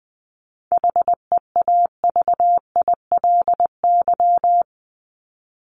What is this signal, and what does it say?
Morse code, heard as a keyed tone, about 20 wpm, 700 Hz. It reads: HEAVILY
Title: LY Suffix - Top 100 Words with Base Words 20wpm